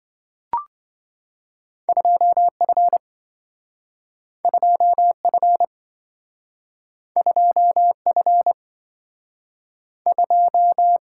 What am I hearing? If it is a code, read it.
Morse code, heard as a keyed tone, about 27 wpm, 700 Hz. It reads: 2F 2F 2F 2